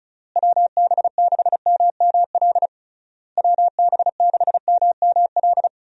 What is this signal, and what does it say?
Morse code, heard as a keyed tone, about 34 wpm, 700 Hz. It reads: WB6MML WB6MML
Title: USA Call Signs - 6 Characters - Repeated 3x - 35wpm